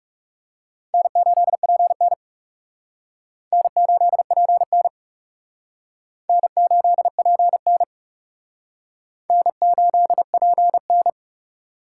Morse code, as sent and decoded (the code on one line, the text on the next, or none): -. ---.. .--. -. / -. ---.. .--. -. / -. ---.. .--. -. / -. ---.. .--. -.
N8PN N8PN N8PN N8PN